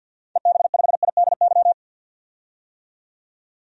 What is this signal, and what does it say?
Morse code, heard as a keyed tone, about 50 wpm, 700 Hz. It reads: EB5IDY
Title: DXCC-100 Call Signs - 6 Characters - 50wpm